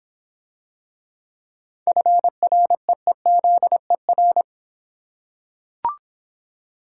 Morse code, heard as a keyed tone, about 26 wpm, 700 Hz. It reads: FREEZER